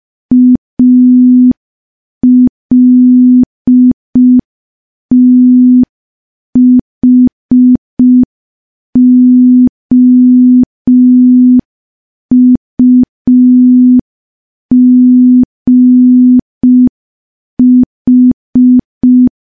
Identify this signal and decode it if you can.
Morse code, heard as a keyed tone, about 5 wpm, 250 Hz. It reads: ALTHOUGH
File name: morse / morse-training-05WPM